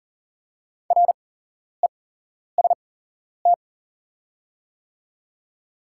Morse code, heard as a keyed tone, about 40 wpm, spaced as for 11 wpm, 690 Hz. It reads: REST
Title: 8x Character Spacing - Top 500 Words - 40wpm